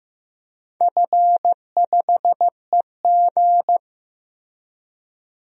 Morse code, heard as a keyed tone, about 14 wpm, 700 Hz. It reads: F5EG
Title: DXCC-100 Call Signs - 4 Characters - Repeated 3x - 15wpm